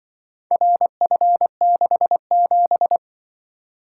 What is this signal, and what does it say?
Morse code, heard as a keyed tone, about 25 wpm, 700 Hz. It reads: RF67
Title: Speed-Racing - Four Letter-Number Combinations - 20wpm